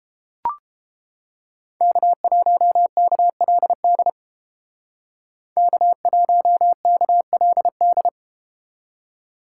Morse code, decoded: K1KLD K1KLD

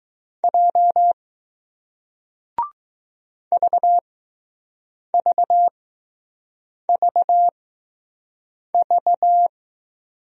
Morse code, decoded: J VVVV

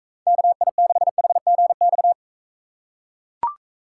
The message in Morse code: KI6HGX

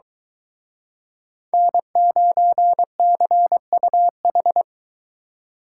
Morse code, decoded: N9CUH

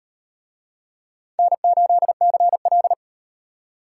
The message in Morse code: N8CL

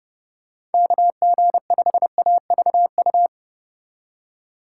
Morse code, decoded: KG5AVU

